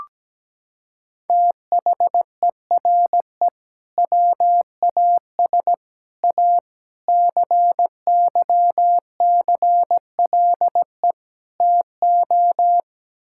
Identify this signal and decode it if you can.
Morse code, heard as a keyed tone, about 17 wpm, 700 Hz. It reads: THERE WAS A CYCLE TO
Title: Sentences from Top 1000 Words - Review Part 6 - 17wpm